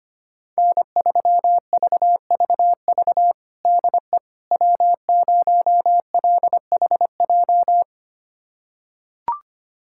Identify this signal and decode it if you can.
Morse code, heard as a keyed tone, about 25 wpm, 700 Hz. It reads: N3VVV DE W0LHJ
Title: USA Callsign DE Callsign - 5 Characters - 25wpm